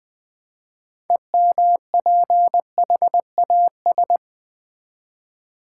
Morse code, heard as a keyed tone, about 20 wpm, 700 Hz. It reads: EMPHAS
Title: Build A Word - 3 Letters At A Time - 9-Letter Words - 20wpm